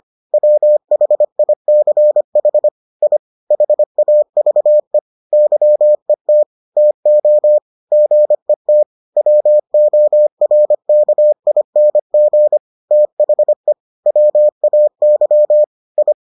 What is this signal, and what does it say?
Morse code, heard as a keyed tone, about 25 wpm, 600 Hz. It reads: WHICH I HAVE YET TO GET WORKING THE WAY I